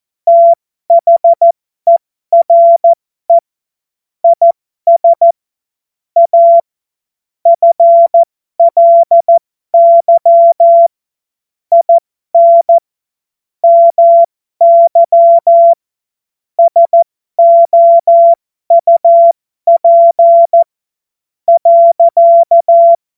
Morse code, heard as a keyed tone, about 14 wpm, spaced as for 11 wpm, 680 Hz. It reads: THERE IS A FLY IN MY SOUP.
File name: Morsekurs / Witze